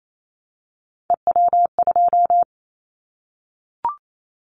Morse code, heard as a keyed tone, about 28 wpm, 700 Hz. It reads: EW2
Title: Callsign Prefixes for Top 100 DXCC - 28wpm